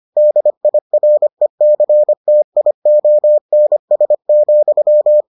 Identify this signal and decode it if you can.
Morse code, heard as a keyed tone, about 25 wpm, 600 Hz. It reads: DIRECTIONS,